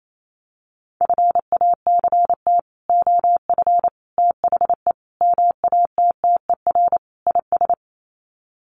Morse code, heard as a keyed tone, about 28 wpm, 700 Hz. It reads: FACT OF THE MATTER IS
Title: Sets of 5 Words 28wpm